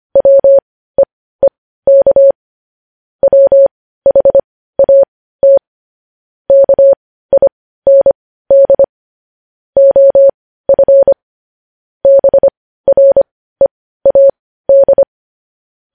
Morse code, heard as a keyed tone, about 25 wpm, spaced as for 15 wpm, 550 Hz. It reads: WEEK WHAT KIND OF BREAD
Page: News Headlines in Morse Code at 15 WPM